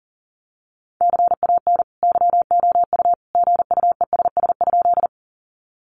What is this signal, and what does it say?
Morse code, heard as a keyed tone, about 40 wpm, 700 Hz. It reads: CAN YOU GUESS?
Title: Sentences from Top 1500 Words 40wpm